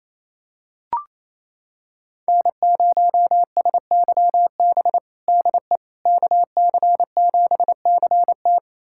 Morse code, heard as a keyed tone, about 28 wpm, 700 Hz. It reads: N0SYB DE KC7CT
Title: USA Callsign DE Callsign - 5 Characters - 28wpm